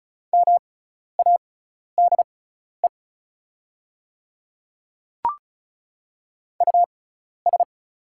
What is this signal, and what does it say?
Morse code, heard as a keyed tone, about 34 wpm, spaced as for 12 wpm, 690 Hz. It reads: MADE US
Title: 6x Character Spacing - Top 500 Words - 35wpm